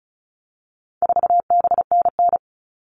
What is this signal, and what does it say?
Morse code, heard as a keyed tone, about 34 wpm, 700 Hz. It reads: 4BNN
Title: All Cut Numbers - 4-Digit 35wpm